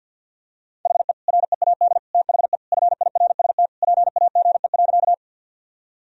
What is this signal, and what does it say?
Morse code, heard as a keyed tone, about 50 wpm, 700 Hz. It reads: HE READ THE FIRST PAGE.
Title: Sentences from Top 1000 Words - Review Part 1 - 50wpm